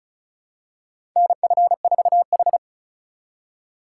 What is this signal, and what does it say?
Morse code, heard as a keyed tone, about 34 wpm, 700 Hz. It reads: NF4H
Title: USA Call Signs - 4 Characters - 2x Character Spacing - 35wpm